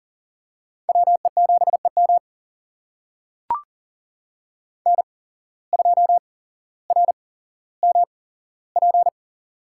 Morse code, heard as a keyed tone, about 39 wpm, spaced as for 17 wpm, 700 Hz. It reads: WE7EM N2RMP
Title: USA Call Signs - 8x Character Spacing - 40wpm